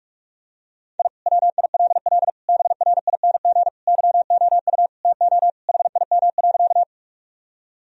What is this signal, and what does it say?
Morse code, heard as a keyed tone, about 46 wpm, 700 Hz. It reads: I WILL BRING YOU TO HIM.
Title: Sentences from Top 1000 Words - Review Part 3 - 45wpm